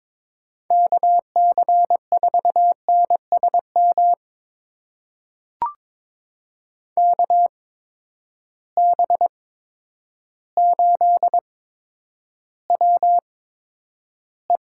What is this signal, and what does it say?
Morse code, heard as a keyed tone, about 22 wpm, spaced as for 10 wpm, 700 Hz. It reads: KC4NSM KB8WE
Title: USA Call Signs - 6 Characters - 8x Character Spacing - 22wpm